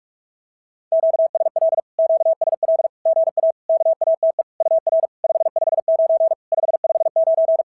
Morse code, heard as a keyed tone, about 46 wpm, 650 Hz. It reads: QSL QSL GA KATE UR 559 559
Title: SOTA QSO Practice 45wpm